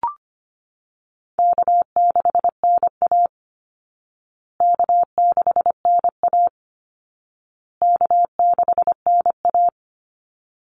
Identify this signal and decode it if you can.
Morse code, heard as a keyed tone, about 25 wpm, 700 Hz. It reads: K6NA K6NA K6NA
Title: Easy Call Signs Repeated 3x - 25wpm